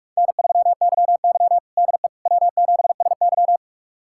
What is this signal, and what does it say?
Morse code, heard as a keyed tone, about 42 wpm, 700 Hz. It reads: N3YY DE W7SY
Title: USA Callsign DE Callsign - 4 Characters - 45wpm